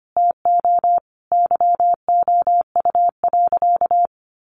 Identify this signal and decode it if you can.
Morse code, heard as a keyed tone, about 25 wpm, 700 Hz. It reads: TO YOU.
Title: Sentences from Top 500 Words 25wpm